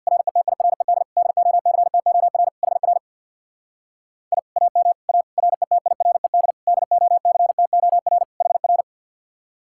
Morse code, heard as a keyed tone, about 62 wpm, 700 Hz. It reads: RETIRED DOCTOR HR I AM A RETIRED DOCTOR HR
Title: Speed-Racing Common QSO Phrases - 50wpm